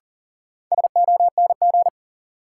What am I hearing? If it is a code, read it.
Morse code, heard as a keyed tone, about 38 wpm, 700 Hz. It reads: SONG
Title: Sentences from Top 400 Words - New Word Drill - 40wpm